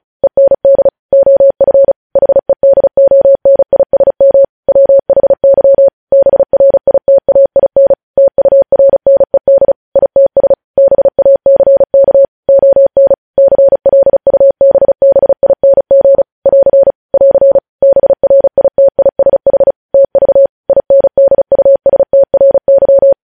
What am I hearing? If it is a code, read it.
Morse code, heard as a keyed tone, about 36 wpm, 550 Hz. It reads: END OF HEDONISM WHY BRITAIN TURNED ITS BACK ON CLUBBING <AR> <AR> BRITISH TV INDUSTRY
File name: podcast / News headlines-1x-35x35